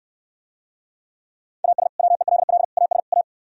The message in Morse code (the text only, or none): AS WELL AS A